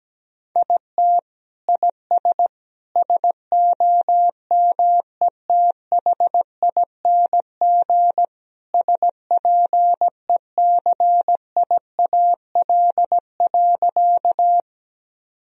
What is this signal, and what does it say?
Morse code, heard as a keyed tone, about 17 wpm, 700 Hz. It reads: IT IS SOMETHING SPECIAL.
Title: Sentences from Top 1000 Words - Review Part 3 - 17wpm